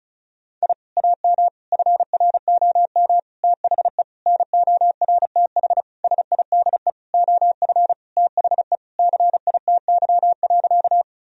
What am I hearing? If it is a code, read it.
Morse code, heard as a keyed tone, about 36 wpm, 700 Hz. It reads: I AM FROM THE NORTH SIDE OF THE CITY.
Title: Sentences from Top 1000 Words - Review Part 1 - 35wpm